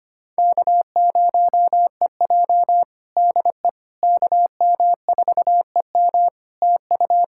Morse code, heard as a keyed tone, about 25 wpm, 700 Hz. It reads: K0EJ DE KM4EM TU
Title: Practice QSO - Exchange 4 - Ending - 25wpm